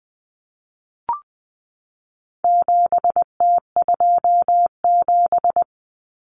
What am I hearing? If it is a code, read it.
Morse code, heard as a keyed tone, about 20 wpm, 700 Hz. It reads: 7T27
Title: Common Cut Numbers - 4-Digit 20wpm